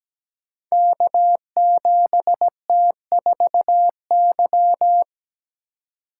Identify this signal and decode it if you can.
Morse code, heard as a keyed tone, about 17 wpm, 700 Hz. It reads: K7T4Y
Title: Code Groups - 5 Letter:Numbers - 17wpm